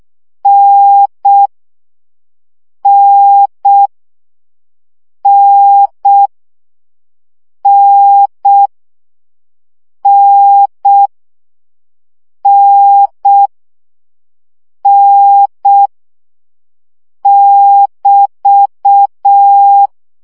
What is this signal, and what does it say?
Morse code, heard as a keyed tone, about 6 wpm, 800 Hz. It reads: N N N N N N N <BT>
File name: CW gyakorlas n 5 baud